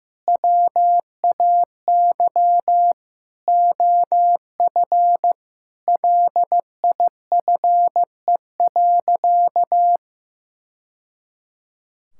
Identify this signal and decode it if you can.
Morse code, heard as a keyed tone, about 15 wpm, 700 Hz. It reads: WAY OF LIFE.
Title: Sentences from Top 1000 Words - Review Part 6 - 15wpm